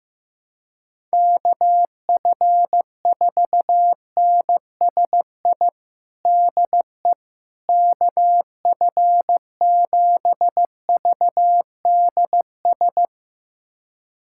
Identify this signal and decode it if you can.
Morse code, heard as a keyed tone, about 15 wpm, 700 Hz. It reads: KF4NSI DE KF7VDS
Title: USA Callsign DE Callsign - 6 Characters - 15wpm